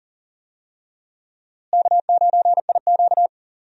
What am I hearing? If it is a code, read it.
Morse code, heard as a keyed tone, about 40 wpm, 700 Hz. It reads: K9IQ
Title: USA Call Signs - 4 Characters - Repeated 2x - 40wpm